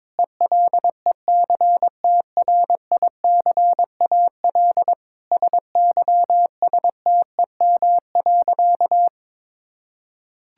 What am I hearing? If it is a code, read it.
Morse code, heard as a keyed tone, about 22 wpm, 700 Hz. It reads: ELECTRICAL SYSTEM.
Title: Sentences from Top 1000 Words - Review Part 6 - 22wpm